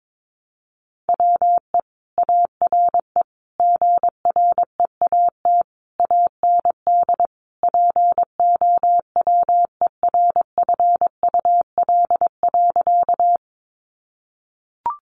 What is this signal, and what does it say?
Morse code, heard as a keyed tone, about 22 wpm, 700 Hz. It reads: WE ARE GREAT AND POWERFUL.
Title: Sentences from Top 1300 Words 22wpm